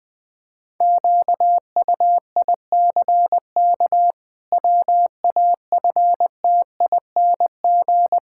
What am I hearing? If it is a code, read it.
Morse code, heard as a keyed tone, about 20 wpm, 700 Hz. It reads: QUICK WAFTING
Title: Pangram Challenge v2 - 20wpm